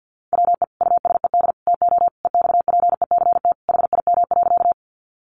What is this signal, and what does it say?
Morse code, heard as a keyed tone, about 50 wpm, 700 Hz. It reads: WE USED TO EXPECT HIM.
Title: Sentences from Top 1100 Words 50wpm